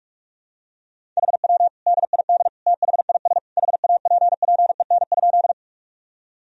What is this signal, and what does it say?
Morse code, heard as a keyed tone, about 42 wpm, 700 Hz. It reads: HW DID THIS HAPPEN?
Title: Sentences from Top 400 Words and Abbreviations 45wpm